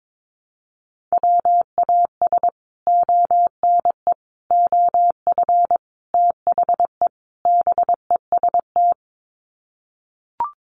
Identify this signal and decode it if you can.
Morse code, heard as a keyed tone, about 22 wpm, 700 Hz. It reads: WAS ONE OF THE BEST
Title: Sets of 5 Words Encore2 22wpm